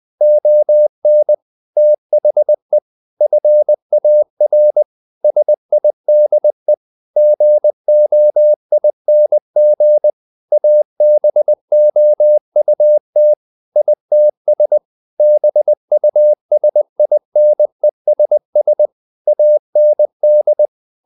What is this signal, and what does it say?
Morse code, heard as a keyed tone, about 20 wpm, 600 Hz. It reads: ON THE FAR SIDE GOING ABOUT ITS BUSINESS AND
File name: the old man and the sea part 2 20wpm